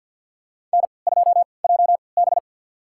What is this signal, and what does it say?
Morse code, heard as a keyed tone, about 50 wpm, spaced as for 28 wpm, 700 Hz. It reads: N2JB
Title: USA Call Signs - 3x Character Spacing - 50wpm